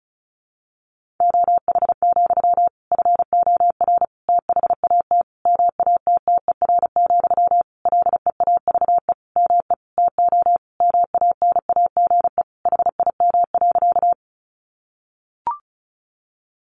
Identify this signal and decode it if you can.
Morse code, heard as a keyed tone, about 34 wpm, 700 Hz. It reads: OH, FOR THAT MATTER, LEAVE ME TO MANAGE HIM.
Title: Sentences from Top 1400 Words 35wpm